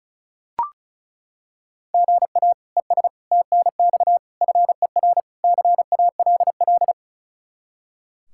Morse code, heard as a keyed tone, about 36 wpm, 700 Hz. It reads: GA ES TNX FER CALL
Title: Practice QSO - Exchange 1 - Beginning - 35wpm